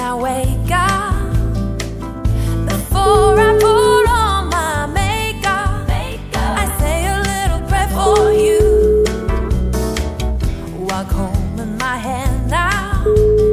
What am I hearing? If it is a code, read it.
Ringback tone (Europe, 425 Hz), with noise.